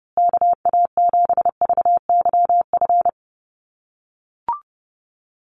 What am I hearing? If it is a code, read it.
Morse code, heard as a keyed tone, about 30 wpm, 700 Hz. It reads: KA7VYF
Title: USA Call Signs Repeated 3x 30wpm